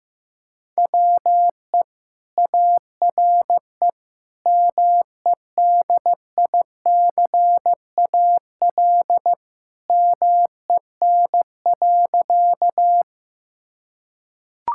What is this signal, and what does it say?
Morse code, heard as a keyed tone, about 15 wpm, 700 Hz. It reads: WE ARE MEDICAL MEN.
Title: Sentences from Top 1000 Words - Review Part 5 - 15wpm